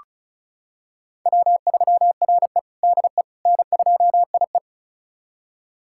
Morse code, heard as a keyed tone, about 36 wpm, 700 Hz. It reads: W3RE DE N2IE
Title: USA Callsign DE Callsign - 4 Characters - 35wpm